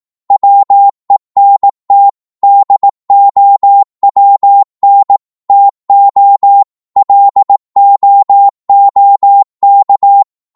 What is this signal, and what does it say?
Morse code, heard as a keyed tone, about 18 wpm, 800 Hz. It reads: WENT DOWN TO LOOK
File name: Morse / psg-18wpm